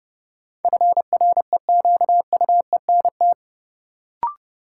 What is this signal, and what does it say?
Morse code, heard as a keyed tone, about 31 wpm, 700 Hz. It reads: FREQUENT
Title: Sentences from Top 1400 Words - New Word Drill - 30wpm